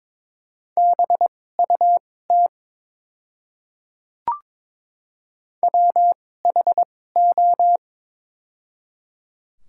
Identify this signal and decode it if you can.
Morse code, heard as a keyed tone, about 21 wpm, spaced as for 16 wpm, 700 Hz. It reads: BUT WHO
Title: 2x Character Spacing - Top 300 Words - 22wpm